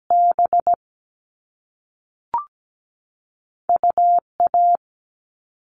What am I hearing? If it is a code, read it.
Morse code, heard as a keyed tone, about 17 wpm, 700 Hz. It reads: B UA
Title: All Cut Numbers - 2-Digit 17wpm